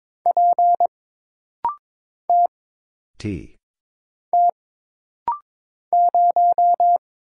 Morse code, heard as a keyed tone, about 22 wpm, 700 Hz. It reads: P T T 0
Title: Single Letter-Number - Rapid-Fire 22wpm